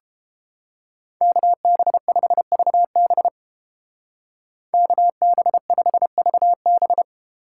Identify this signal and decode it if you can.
Morse code, heard as a keyed tone, about 31 wpm, 700 Hz. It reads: KB5VB KB5VB